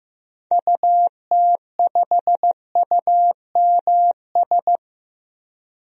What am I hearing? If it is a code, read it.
Morse code, heard as a keyed tone, about 15 wpm, 700 Hz. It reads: UT5UMS
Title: DXCC-100 Call Signs - 6 Characters - 15wpm